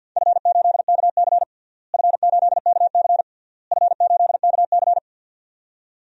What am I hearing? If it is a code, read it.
Morse code, heard as a keyed tone, about 46 wpm, 700 Hz. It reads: F8KC F8KC F8KC
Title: DXCC-100 Call Signs - 4 Characters - Repeated 3x - 50wpm